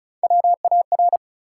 Morse code, heard as a keyed tone, about 36 wpm, 700 Hz. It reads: WAR
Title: Three Letter Words - 35wpm